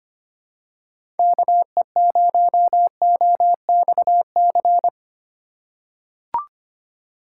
Morse code, heard as a keyed tone, about 25 wpm, 700 Hz. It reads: KE0OXC
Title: USA Call Signs - 6 Characters - 4x Character Spacing - 25wpm